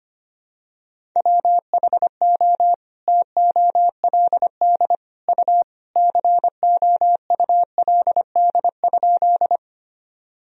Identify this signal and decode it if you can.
Morse code, heard as a keyed tone, about 25 wpm, 700 Hz. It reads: WHO TOLD U COULD?
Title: Sentences from Top 400 Words and Abbreviations 25wpm